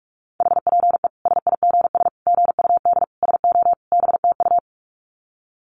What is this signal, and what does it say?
Morse code, heard as a keyed tone, about 44 wpm, 700 Hz. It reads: HPE SIGS GUD SO BTU
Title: Common QSO Phrases 45wpm